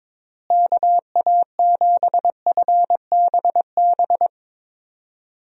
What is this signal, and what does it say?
Morse code, heard as a keyed tone, about 21 wpm, 700 Hz. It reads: KA7FBB
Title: USA Call Signs - 6 Characters - Repeated 2x - 22wpm